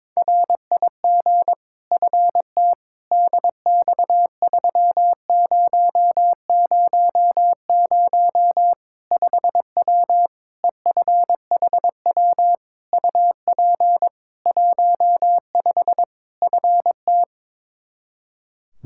Morse code, heard as a keyed tone, about 22 wpm, 700 Hz. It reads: RIG FT DX3000 5W EFHW UP 15 FT
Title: K1USN SST QSO Practice 22wpm